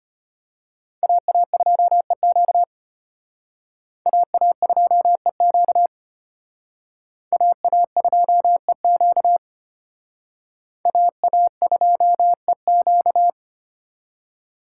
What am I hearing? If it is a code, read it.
Morse code, heard as a keyed tone, about 33 wpm, 700 Hz. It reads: AA2EQ AA2EQ AA2EQ AA2EQ